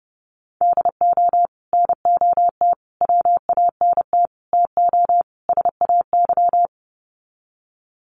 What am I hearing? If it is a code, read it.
Morse code, heard as a keyed tone, about 30 wpm, 700 Hz. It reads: DO NOT WANT TO SAY
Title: Sets of 5 Words Encore 30wpm